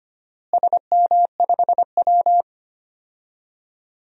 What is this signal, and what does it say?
Morse code, heard as a keyed tone, about 25 wpm, 700 Hz. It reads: SM5W
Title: DXCC-100 Call Signs - 4 Characters - Repeated 3x - 25wpm